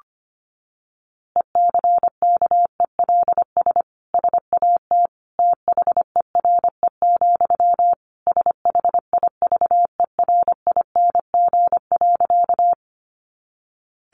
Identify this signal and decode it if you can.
Morse code, heard as a keyed tone, about 25 wpm, 700 Hz. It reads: ECKELS SAT THERE, SHIVERING.
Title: A Sound Of Thunder 25wpm